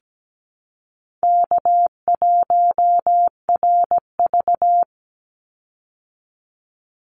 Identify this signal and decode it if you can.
Morse code, heard as a keyed tone, about 17 wpm, 700 Hz. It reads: K1RV
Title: Call Signs Repeated 3x - 17wpm